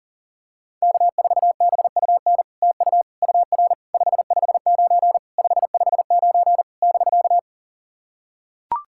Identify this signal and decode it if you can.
Morse code, heard as a keyed tone, about 40 wpm, 700 Hz. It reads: K4BUN TU UR 559 559 <BK>
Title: SOTA QSO Practice 40wpm